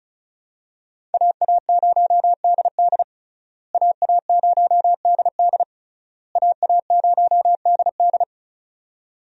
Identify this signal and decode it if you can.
Morse code, heard as a keyed tone, about 36 wpm, 700 Hz. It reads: AA0DD AA0DD AA0DD
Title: USA Call Signs - 5 Characters - Repeated 3x - 35wpm